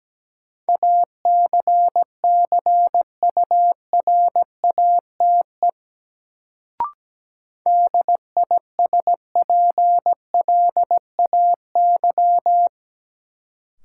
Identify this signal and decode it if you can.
Morse code, heard as a keyed tone, about 17 wpm, 700 Hz. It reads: ACCURATE DISPLAY
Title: Sentences from Top 1200 Words - New Word Drill - 17wpm